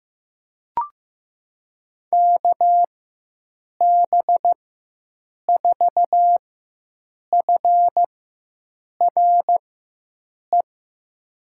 Morse code, heard as a keyed tone, about 15 wpm, spaced as for 7 wpm, 700 Hz. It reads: KB4FRE
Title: USA Call Signs - 6 Characters - 4x Character Spacing - 15wpm